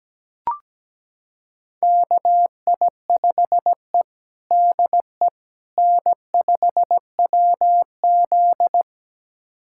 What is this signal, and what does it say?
Morse code, heard as a keyed tone, about 17 wpm, 700 Hz. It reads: KI5E DE N5WZ
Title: USA Callsign DE Callsign - 4 Characters - 17wpm